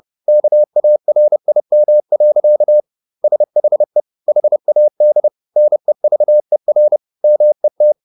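Morse code, heard as a keyed tone, about 30 wpm, 600 Hz. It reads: KARIM. SHE HAD NEVER MET
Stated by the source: Morse code at 30 wpm, the novel Never by Ken Follett in English, from CW med Gnister.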